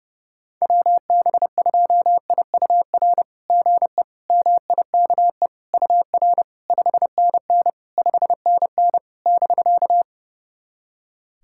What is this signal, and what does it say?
Morse code, heard as a keyed tone, about 32 wpm, 690 Hz. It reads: WB2IUR GE MIKE UR 5NN 5NN <BK>
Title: SOTA QSO Practice 30wpm